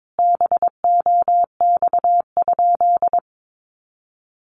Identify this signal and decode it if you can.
Morse code, heard as a keyed tone, about 22 wpm, 700 Hz. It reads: BOX?
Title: Sentences from Top 1200 Words 22wpm